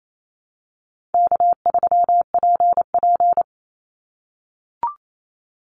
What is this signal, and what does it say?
Morse code, heard as a keyed tone, about 28 wpm, 700 Hz. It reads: K3PP
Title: Easy Call Signs - 28wpm